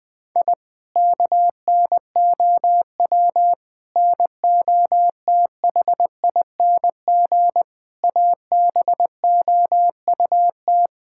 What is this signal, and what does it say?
Morse code, heard as a keyed tone, about 21 wpm, 700 Hz. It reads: I KNOW NOTHING ABOUT
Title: Sentences from Top 1000 Words - Review Part 6 - 20wpm